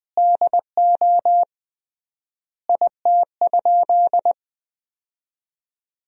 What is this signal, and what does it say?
Morse code, heard as a keyed tone, about 20 wpm, 700 Hz. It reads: DO IT?
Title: 3x Word-Spacing - Sentences from Top 100 Words - 20wpm